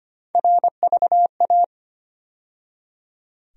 Morse code, heard as a keyed tone, about 25 wpm, 700 Hz. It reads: RVA
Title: USA Call Sign Postfixes - 3 Letters 25wpm